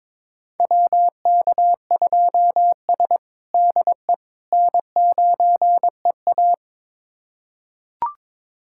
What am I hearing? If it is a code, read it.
Morse code, heard as a keyed tone, about 21 wpm, 700 Hz. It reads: WK2S DE N9EA
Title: USA Callsign DE Callsign - 4 Characters - 22wpm